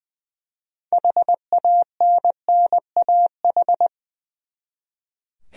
Morse code, heard as a keyed tone, about 20 wpm, 700 Hz. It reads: HANNAH
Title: USA Common Names - 20wpm